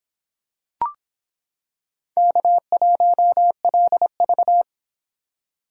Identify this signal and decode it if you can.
Morse code, heard as a keyed tone, about 25 wpm, 700 Hz. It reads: K1LV